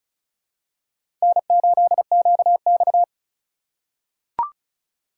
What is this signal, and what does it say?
Morse code, heard as a keyed tone, about 36 wpm, 700 Hz. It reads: N8QX
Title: USA Call Signs - 4 Characters - 3x Character Spacing - 35wpm